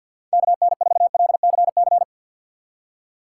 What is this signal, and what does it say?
Morse code, heard as a keyed tone, about 53 wpm, 700 Hz. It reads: KN4LCC
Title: USA Call Signs - 6 Characters - 4x Character Spacing - 50wpm